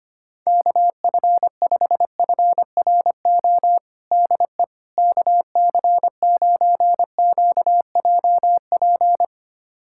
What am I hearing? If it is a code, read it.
Morse code, heard as a keyed tone, about 25 wpm, 700 Hz. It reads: KF5FRO DE KC9QJP
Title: USA Callsign DE Callsign - 6 Characters - 25wpm